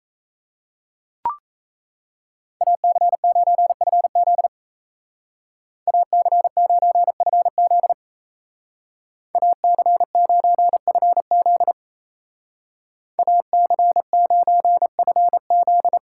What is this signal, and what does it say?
Morse code, heard as a keyed tone, about 34 wpm, 700 Hz. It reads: AC9FZ AC9FZ AC9FZ AC9FZ
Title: Speed Racing - USA Call Signs 28wpm